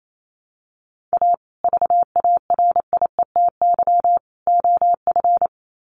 Morse code, heard as a keyed tone, about 28 wpm, 700 Hz. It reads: A VARIETY OF
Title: Sets of 3 Words 28wpm